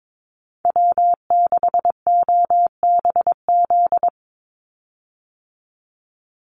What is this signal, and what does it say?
Morse code, heard as a keyed tone, about 22 wpm, 700 Hz. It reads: W6OBZ
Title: USA Call Signs 22wpm